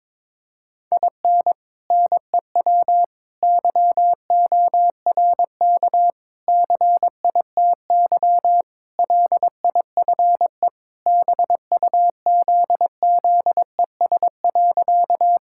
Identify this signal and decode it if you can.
Morse code, heard as a keyed tone, about 22 wpm, 700 Hz. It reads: IN NEW YORK CITY LIFE BUZZES.
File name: practice / 22wpm / Sentences with 4-Word Phrases - 22wpm